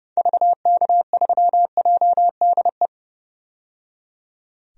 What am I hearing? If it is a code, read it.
Morse code, heard as a keyed tone, about 30 wpm, 700 Hz. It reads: VK3JDE
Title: DXCC-100 Call Signs - 6 Characters - 30wpm